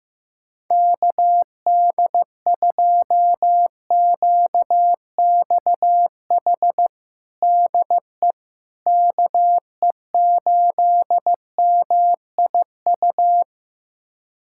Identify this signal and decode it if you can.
Morse code, heard as a keyed tone, about 15 wpm, 700 Hz. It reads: KD2QXH DE KE8MIU
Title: USA Callsign DE Callsign - 6 Characters - 15wpm